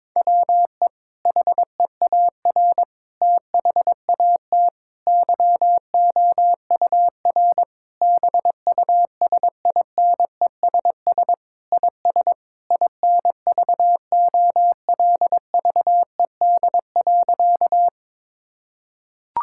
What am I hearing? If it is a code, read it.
Morse code, heard as a keyed tone, about 22 wpm, 700 Hz. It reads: WE HEAR THAT YOUR BUSINESS IS INVOLVED.
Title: Sentences from Top 1000 Words - Review Part 5 - 22wpm